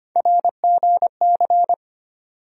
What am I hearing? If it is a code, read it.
Morse code, heard as a keyed tone, about 25 wpm, 700 Hz. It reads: RGC
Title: USA Call Sign Postfixes - 3 Letters 25wpm